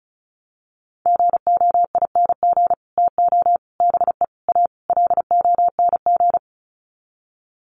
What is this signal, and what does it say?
Morse code, heard as a keyed tone, about 34 wpm, 700 Hz. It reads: GOING TO BE A LONG
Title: Sets of 5 Words Encore2 35wpm